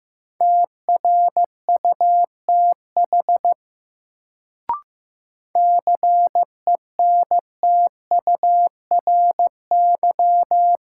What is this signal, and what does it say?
Morse code, heard as a keyed tone, about 15 wpm, 700 Hz. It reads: TRUTH CENTURY
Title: Sets of 5 Words - New Word Drill - 15wpm